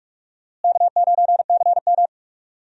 Morse code, heard as a keyed tone, about 45 wpm, 700 Hz. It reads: K9CM
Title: USA Call Signs - 4 Characters - Repeated 2x - 45wpm